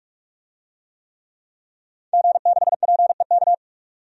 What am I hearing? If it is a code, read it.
Morse code, heard as a keyed tone, about 44 wpm, 700 Hz. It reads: G6PEK
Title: DXCC-100 Call Signs - 5 Characters - 45wpm